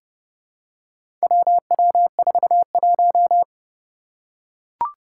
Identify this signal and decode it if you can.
Morse code, heard as a keyed tone, about 30 wpm, 700 Hz. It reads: WW41
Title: Four Challenging Letter-Number Combinations 30wpm